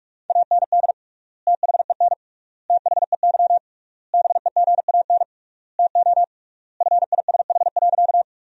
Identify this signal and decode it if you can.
Morse code, heard as a keyed tone, about 43 wpm, 700 Hz. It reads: AND THEN THEY BEGAN TO FISH.
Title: 3x Word-Spacing - Sentences from Top 300 Words - 45wpm